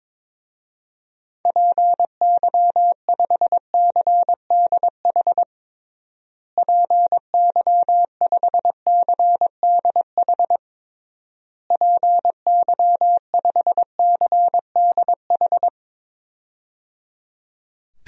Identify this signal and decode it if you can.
Morse code, heard as a keyed tone, about 22 wpm, 700 Hz. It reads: PY5CDH PY5CDH PY5CDH
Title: DXCC-100 Call Signs - 6 Characters - Repeated 3x - 22wpm